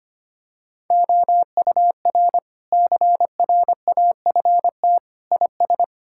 Morse code, heard as a keyed tone, about 25 wpm, 700 Hz. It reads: OUR CRAFT IS
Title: Sentences from Top 1000 Words - Review Part 6 - 25wpm